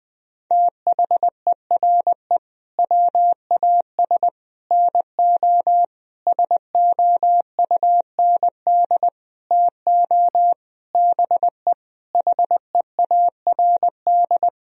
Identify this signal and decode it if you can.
Morse code, heard as a keyed tone, about 20 wpm, 700 Hz. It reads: THERE WAS NO SOUND TO BE HEARD
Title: Sentences from Top 1000 Words - Review Part 3 - 20wpm